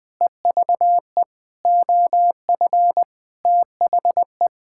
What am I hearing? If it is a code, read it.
Morse code, heard as a keyed tone, about 21 wpm, 700 Hz. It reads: EVE OF THE
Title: Sentences with 5-Word Phrases - 20wpm